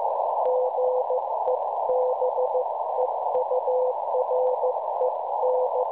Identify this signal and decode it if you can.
Morse code, heard as a keyed tone, about 15 wpm, 510 Hz, with noise. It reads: GEBEUREN
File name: cwronde / websdr recording .6kHz